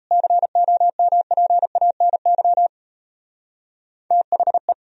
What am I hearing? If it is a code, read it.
Morse code, heard as a keyed tone, about 33 wpm, 700 Hz. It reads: COMPANY THE
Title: Speed-Racing - Sets of 2 Words 28wpm